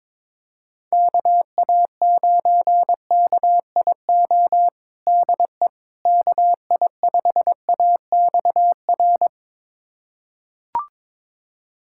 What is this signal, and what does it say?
Morse code, heard as a keyed tone, about 22 wpm, 700 Hz. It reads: KA9KIO DE KI5AXR
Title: USA Callsign DE Callsign - 6 Characters - 22wpm